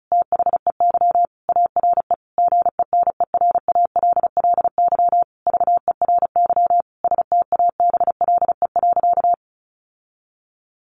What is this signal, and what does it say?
Morse code, heard as a keyed tone, about 34 wpm, 700 Hz. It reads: THEY ARE GENERALLY VERY STABLE.
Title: Sentences from Top 1500 Words 35wpm